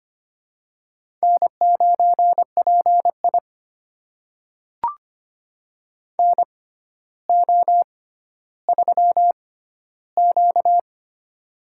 Morse code, heard as a keyed tone, about 25 wpm, spaced as for 13 wpm, 700 Hz. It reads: N9PI NO3Q
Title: USA Call Signs - 6x Character Spacing - 25wpm